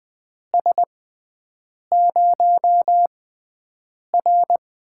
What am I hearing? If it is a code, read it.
Morse code, heard as a keyed tone, about 20 wpm, spaced as for 7 wpm, 700 Hz. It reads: S0R
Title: USA Call Signs - 6x Character Spacing - 20wpm